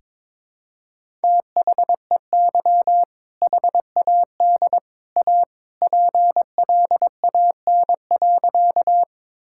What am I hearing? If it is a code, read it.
Morse code, heard as a keyed tone, about 22 wpm, 700 Hz. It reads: THEY HAD A PLAN.
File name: practice / 22wpm / Sentences with 2-Word Phrases - 22wpm